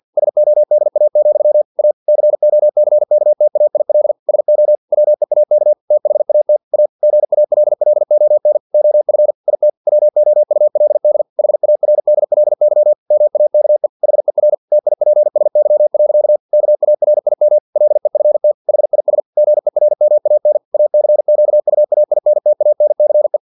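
Morse code, heard as a keyed tone, about 50 wpm, 600 Hz. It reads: SODA, A COCKTAIL SO WEAK THAT A GALLON OF IT WOULD HARDLY MAKE HER TIPSY. KARIM LEFT HIS GERMAN ACQUAINTANCE
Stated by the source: Morse practice text from an English-language novel at 49 wpm.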